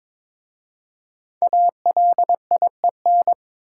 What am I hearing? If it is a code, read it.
Morse code, heard as a keyed tone, about 23 wpm, 700 Hz. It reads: ALIEN
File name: 5 Letter Words - First 3 Letters Given - 22wpm